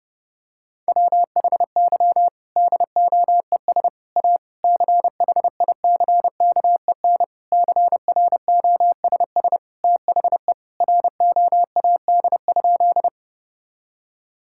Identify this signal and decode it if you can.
Morse code, heard as a keyed tone, about 30 wpm, 700 Hz. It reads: WHY DOES A CHICKEN CROSS THE ROAD?
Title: Sentences from Top 1000 Words - Review Part 5 - 30wpm